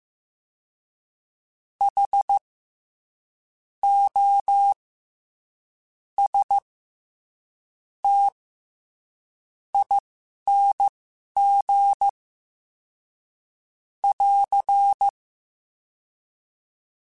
Morse code, heard as a keyed tone, about 15 wpm, spaced as for 11 wpm, 800 Hz. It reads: H O S T ING <AR>
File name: Special INGs-mp3s / Special INGs-mp3s / SlowToFaster ing @10